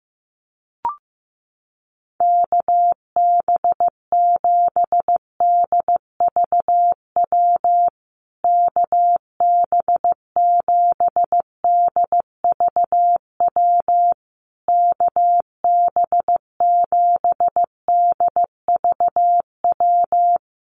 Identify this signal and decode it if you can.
Morse code, heard as a keyed tone, about 15 wpm, 700 Hz. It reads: KB7DVW KB7DVW KB7DVW
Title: USA Call Signs Repeated 3x 15wpm